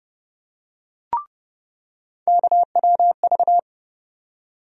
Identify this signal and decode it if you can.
Morse code, heard as a keyed tone, about 31 wpm, 700 Hz. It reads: KWV
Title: USA Call Sign Postfixes - 3 Letters 30wpm